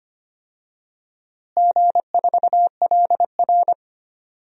Morse code, heard as a keyed tone, about 25 wpm, 700 Hz. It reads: G4LR
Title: DXCC-100 Call Signs - 4 Characters - Repeated 2x - 25wpm